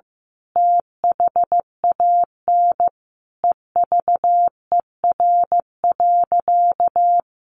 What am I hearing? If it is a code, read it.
Morse code, heard as a keyed tone, about 15 wpm, 700 Hz. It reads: THAN EVER.
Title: Sentences from Top 1200 Words 15wpm